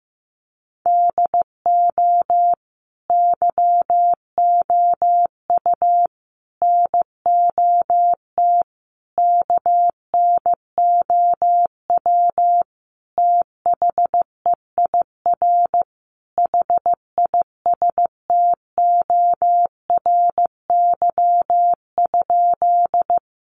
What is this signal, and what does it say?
Morse code, heard as a keyed tone, about 15 wpm, 700 Hz. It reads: DO YOU NOT KNOW THEIR HISTORY?
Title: Sentences from Top 600 Words 15wpm